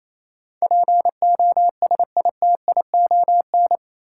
Morse code, heard as a keyed tone, about 28 wpm, 700 Hz. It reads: POSITION
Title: Sentences from Top 900 Words - New Word Drill - 28wpm